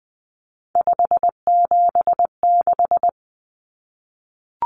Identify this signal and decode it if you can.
Morse code, heard as a keyed tone, about 20 wpm, 700 Hz. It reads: 576
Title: Numbers - 3-Digit 20wpm